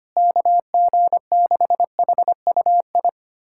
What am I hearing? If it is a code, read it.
Morse code, heard as a keyed tone, about 25 wpm, 700 Hz. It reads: KG6HUI
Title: USA Callsign DE Callsign 25wpm